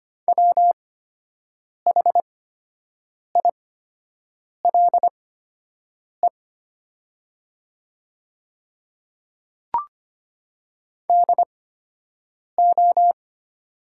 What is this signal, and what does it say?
Morse code, heard as a keyed tone, about 25 wpm, spaced as for 7 wpm, 700 Hz. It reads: WHILE DO
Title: 8x Character Spacing - Top 300 Words - 25wpm